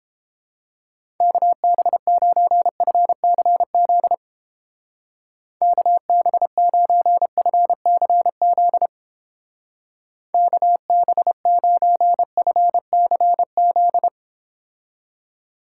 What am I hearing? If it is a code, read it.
Morse code, heard as a keyed tone, about 30 wpm, 700 Hz. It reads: KB9FCZ KB9FCZ KB9FCZ